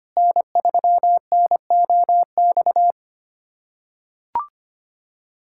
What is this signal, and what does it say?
Morse code, heard as a keyed tone, about 25 wpm, 700 Hz. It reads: N3NOX
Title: USA Callsign DE Callsign - 5 Characters - 25wpm